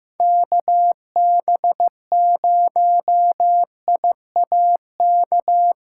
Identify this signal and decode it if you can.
Morse code, heard as a keyed tone, about 15 wpm, 700 Hz. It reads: KB0I AK
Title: USA Call Signs - 8x Character Spacing - 15wpm